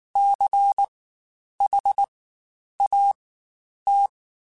Morse code, heard as a keyed tone, about 19 wpm, spaced as for 9 wpm, 800 Hz. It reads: CHAT
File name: Spaced Alphas / Spaced Alphas mp3s / Cs sp @18